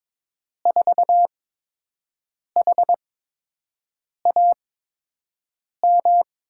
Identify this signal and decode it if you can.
Morse code, heard as a keyed tone, about 22 wpm, spaced as for 6 wpm, 700 Hz. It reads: 4HAM
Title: USA Call Signs - 6 Characters - 8x Character Spacing - 22wpm